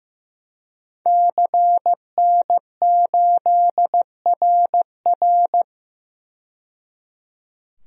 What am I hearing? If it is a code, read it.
Morse code, heard as a keyed tone, about 15 wpm, 700 Hz. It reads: CN8RR
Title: DXCC-100 Call Signs - 5 Characters - 15wpm